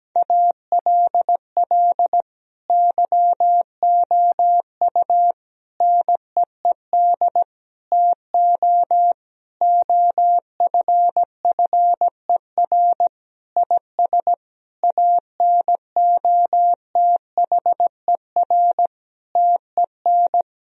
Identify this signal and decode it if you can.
Morse code, heard as a keyed tone, about 17 wpm, 700 Hz. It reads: ALL YOU NEED TO OFFER IS ANOTHER TEN
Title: Sentences from Top 1000 Words - Review Part 4 - 17wpm